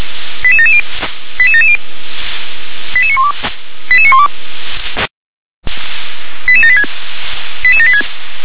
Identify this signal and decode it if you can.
Five-tone selective call (ZVEI), heard as quick runs of five tones, tones of about 70 ms. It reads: 88770 88770 88912 88912 88786 88786